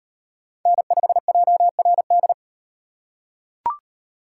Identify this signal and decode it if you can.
Morse code, heard as a keyed tone, about 35 wpm, 700 Hz. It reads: N5JRD